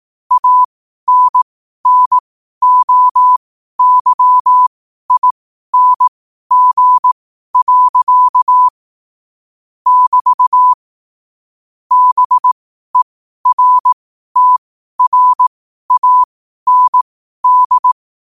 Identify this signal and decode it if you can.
Morse code, heard as a keyed tone, about 18 wpm, spaced as for 12 wpm, 1000 Hz. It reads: ANNOYING. <BT> BERTRAND
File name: qotdfiles / 12wpm